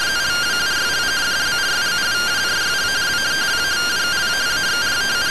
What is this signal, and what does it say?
Two-tone teleprinter signal, about 170 Hz shift.
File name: findings / MR AXIAL 0mm 6mm.gen-08 rtty rx